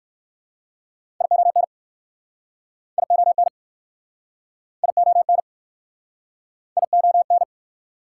Morse code, heard as a keyed tone, about 60 wpm, 700 Hz. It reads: ION ION ION ION